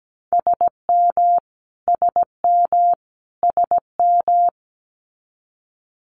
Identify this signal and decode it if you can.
Morse code, heard as a keyed tone, about 17 wpm, 700 Hz. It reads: SM SM SM
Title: Multi-letter Phonemes Repeated 3x 17wpm